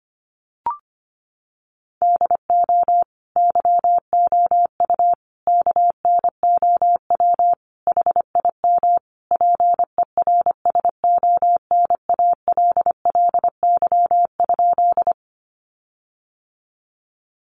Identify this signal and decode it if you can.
Morse code, heard as a keyed tone, about 25 wpm, 700 Hz. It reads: DO YOU KNOW HIM PERSONALLY?
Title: Sentences from Top 1400 Words 25wpm